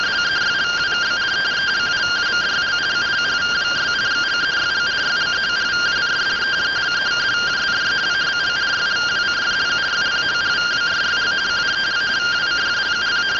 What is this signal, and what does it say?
Two-tone teleprinter signal, about 170 Hz shift.
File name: findings / 81 axial iodine 262.gen-09 rtty rx